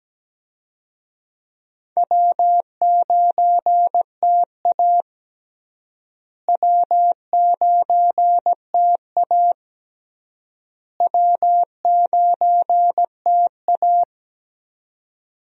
Morse code, heard as a keyed tone, about 17 wpm, 700 Hz. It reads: W9TA W9TA W9TA
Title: USA Call Signs - 4 Characters - Repeated 3x - 17wpm